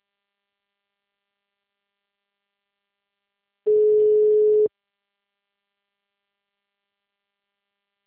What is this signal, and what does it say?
Ringback tone (Europe, 425 Hz).
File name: out- -2004- - - .94